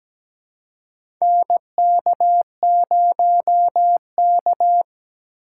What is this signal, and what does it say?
Morse code, heard as a keyed tone, about 17 wpm, 700 Hz. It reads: NK0K
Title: USA Call Signs - 4x Character Spacing - 17wpm